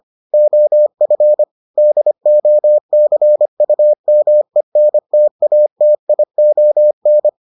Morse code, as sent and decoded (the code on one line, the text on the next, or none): --- ..-. / -.. --- -.-. ..- -- . -. - .- - .. --- -.
OF DOCUMENTATION